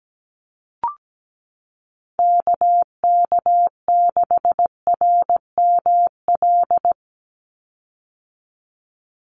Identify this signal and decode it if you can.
Morse code, heard as a keyed tone, about 17 wpm, 700 Hz. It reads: KK6RML
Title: USA Call Signs 17wpm